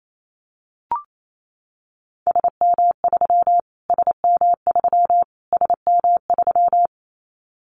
Morse code, heard as a keyed tone, about 28 wpm, 700 Hz. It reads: SM3 SM3 SM3
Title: Callsign Prefixes for Top 25 DXCC - Repeated 3x - 28wpm